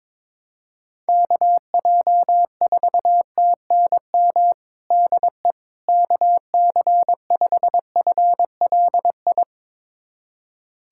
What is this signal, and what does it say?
Morse code, heard as a keyed tone, about 22 wpm, 700 Hz. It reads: KJ4TNM DE KC5FLI
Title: USA Callsign DE Callsign - 6 Characters - 22wpm